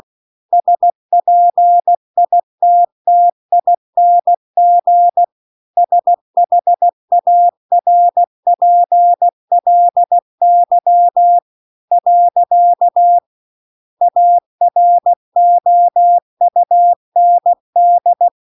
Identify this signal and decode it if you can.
Morse code, heard as a keyed tone, about 16 wpm, 700 Hz. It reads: SPITTING SHARPLY . AROUND
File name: CW-The-Big-Twoheaded-River-2-16WPM